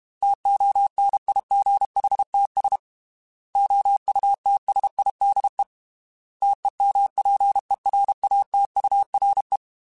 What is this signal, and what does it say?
Morse code, heard as a keyed tone, about 32 wpm, 800 Hz. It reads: TONIGHTS OUTSIDE TEMPERATURE
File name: Weather Sets / Weather Sets-mp3s / Weather 4 17 @30